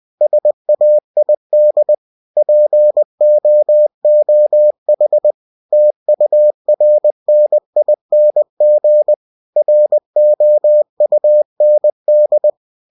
Morse code, heard as a keyed tone, about 20 wpm, 600 Hz. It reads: SAID POOH TURNING ROUND
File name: cw-books / winnie-the-pooh / winnie-20wpm-page-0081